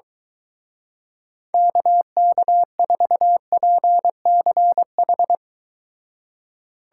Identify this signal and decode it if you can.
Morse code, heard as a keyed tone, about 24 wpm, 700 Hz. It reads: KK4PCH